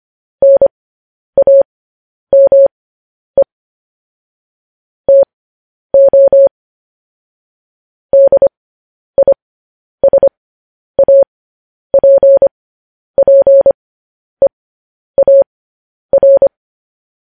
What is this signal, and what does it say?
Morse code, heard as a keyed tone, about 25 wpm, spaced as for 10 wpm, 550 Hz. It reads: NAME TO DISAPPEAR
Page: news-headlines-in-morse-code-at-10-wpm